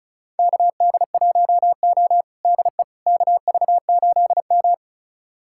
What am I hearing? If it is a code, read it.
Morse code, heard as a keyed tone, about 36 wpm, 690 Hz. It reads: KD1O DE KV8M
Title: USA Callsign DE Callsign - 4 Characters - 35wpm